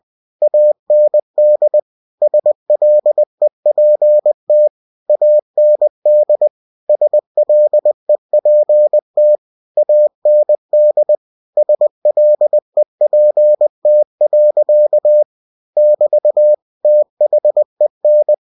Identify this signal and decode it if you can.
Morse code, heard as a keyed tone, about 20 wpm, 600 Hz. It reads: AND SLEPT AND SLEPT AND SLEPT. <BT> THEN
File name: cw-books / winnie-the-pooh / winnie-20wpm-page-0093